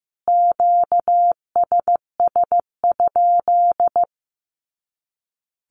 Morse code, heard as a keyed tone, about 15 wpm, 700 Hz. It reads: QSS?
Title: Q-Codes Repeated 3x 15wpm